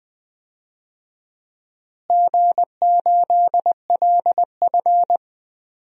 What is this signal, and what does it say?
Morse code, heard as a keyed tone, about 20 wpm, 700 Hz. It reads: G8LF
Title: DXCC-100 Call Signs - 4 Characters - Repeated 3x - 20wpm